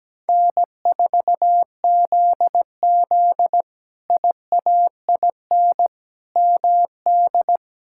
Morse code, heard as a keyed tone, about 17 wpm, 700 Hz. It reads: N4ZZ IAIN MD
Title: K1USN SST QSO Practice - 17wpm